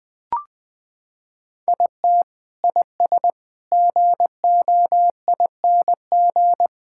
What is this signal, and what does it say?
Morse code, heard as a keyed tone, about 20 wpm, 700 Hz. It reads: IT IS GOING
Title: Sentences with 5-Word Phrases - 20wpm